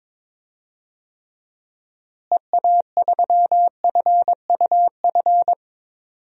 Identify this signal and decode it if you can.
Morse code, heard as a keyed tone, about 22 wpm, 700 Hz. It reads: EA3FUF
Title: DXCC-100 Call Signs - 6 Characters - Repeated 3x - 22wpm